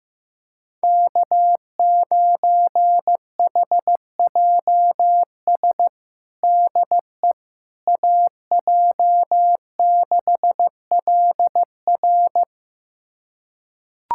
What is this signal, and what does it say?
Morse code, heard as a keyed tone, about 15 wpm, 700 Hz. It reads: K9HJS DE AJ6LR
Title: USA Callsign DE Callsign - 5 Characters - 15wpm